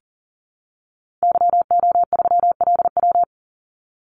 Morse code, heard as a keyed tone, about 40 wpm, 700 Hz. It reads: YO3LW
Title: Call Signs Repeated 2x - 40wpm